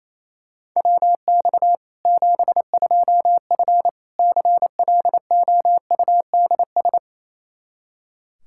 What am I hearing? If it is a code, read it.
Morse code, heard as a keyed tone, about 27 wpm, 700 Hz. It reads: WX 72F CLOUDS
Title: K1USN SST QSO Practice 28wpm